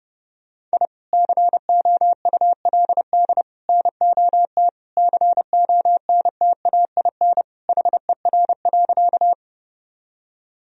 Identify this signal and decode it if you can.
Morse code, heard as a keyed tone, about 30 wpm, 700 Hz. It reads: I COULD NOT CONTAIN HER.
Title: Sentences from Top 1000 Words - Review Part 2 - 30wpm